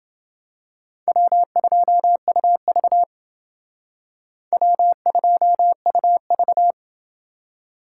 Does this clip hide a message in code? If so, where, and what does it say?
Yes: Morse code from 1.1 to 6.7 s, heard as a keyed tone, about 28 wpm, 700 Hz. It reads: W2UV W2UV